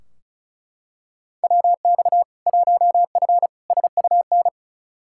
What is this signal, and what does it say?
Morse code, heard as a keyed tone, about 34 wpm, 700 Hz. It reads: WX 1F SUN
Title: Practice QSO - Exchange 2 - Info - 35wpm